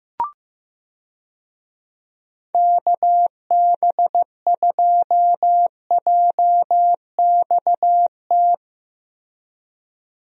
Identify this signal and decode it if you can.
Morse code, heard as a keyed tone, about 15 wpm, 700 Hz. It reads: KB2JXT
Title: USA Call Signs - 6 Characters - 15wpm